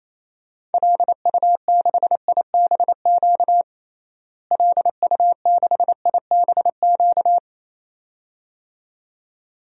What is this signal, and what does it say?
Morse code, heard as a keyed tone, about 28 wpm, 700 Hz. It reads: LU6IBQ LU6IBQ
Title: DXCC-100 Call Signs - 6 Characters - Repeated 2x - 28wpm